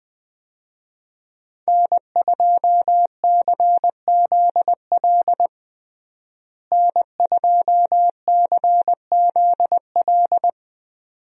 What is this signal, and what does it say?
Morse code, heard as a keyed tone, about 20 wpm, 700 Hz. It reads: N2CZL N2CZL
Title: USA Call Signs - 5 Characters - Repeated 2x - 20wpm